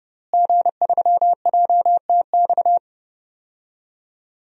Morse code, heard as a keyed tone, about 32 wpm, 700 Hz. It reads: G3JTX
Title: DXCC-100 Call Signs - 5 Characters - Repeated 3x - 30wpm